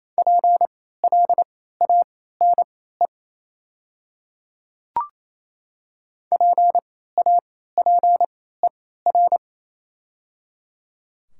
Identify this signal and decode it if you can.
Morse code, heard as a keyed tone, about 28 wpm, spaced as for 16 wpm, 690 Hz. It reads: PLANE PAPER
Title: 3x Character Spacing - Top 500 Words - 28wpm